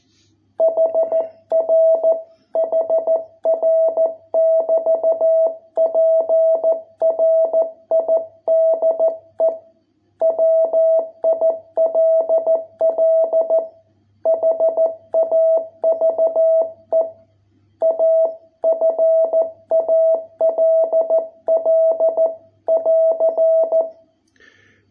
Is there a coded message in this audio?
Yes: Morse code from 0.6 to 23.8 s, heard as a keyed tone, about 13 wpm, 660 Hz. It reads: HR HR <BT> PRIDE WILL HAVE A FALL <AR>